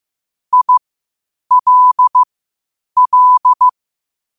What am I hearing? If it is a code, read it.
Morse code, heard as a keyed tone, about 15 wpm, spaced as for 8 wpm, 1000 Hz. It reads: ILL
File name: qotdfiles / 8wpm